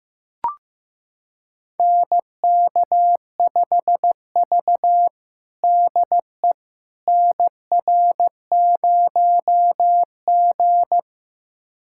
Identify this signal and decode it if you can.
Morse code, heard as a keyed tone, about 16 wpm, 700 Hz. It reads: NK5V DE NR0G
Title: USA Callsign DE Callsign - 4 Characters - 15wpm